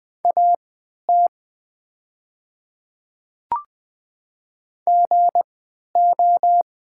Morse code, heard as a keyed tone, about 21 wpm, spaced as for 12 wpm, 700 Hz. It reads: AT GO
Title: 3x Character Spacing - Top 100 Words - 20wpm